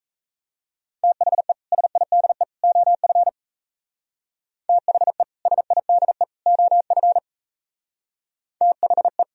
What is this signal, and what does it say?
Morse code, heard as a keyed tone, about 35 wpm, 700 Hz. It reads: THE SIDE OF THE SIDE OF THE